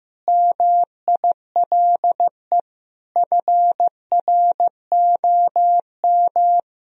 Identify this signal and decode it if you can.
Morse code, heard as a keyed tone, about 14 wpm, 700 Hz. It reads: MILE FROM
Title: Sentences from Top 1000 Words - Review Part 1 - 15wpm